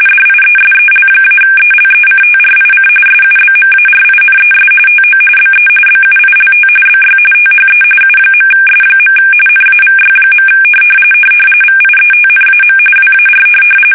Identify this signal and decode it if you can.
RTTY, about 45.45 baud. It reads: DIESEL-ELECTRIC SUBMARINES TO BE BUILT BY THE U.S. NAVY. VIETNAM WAR
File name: fsk850 / usnavy5